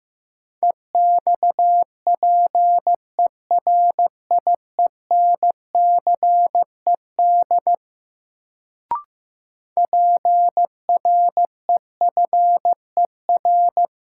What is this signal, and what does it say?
Morse code, heard as a keyed tone, about 15 wpm, 700 Hz. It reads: EXPERIENCED PREFER
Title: Sentences from Top 1100 Words - New Word Drill - 15wpm